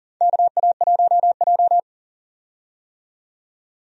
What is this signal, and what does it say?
Morse code, heard as a keyed tone, about 40 wpm, 700 Hz. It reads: KA1J
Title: USA Call Signs - 4 Characters - 40wpm